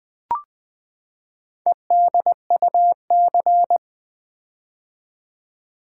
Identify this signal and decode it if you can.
Morse code, heard as a keyed tone, about 21 wpm, 700 Hz. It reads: EDUC
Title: Build A Word - 4 Letters At A Time - 8-Letter Words - 20wpm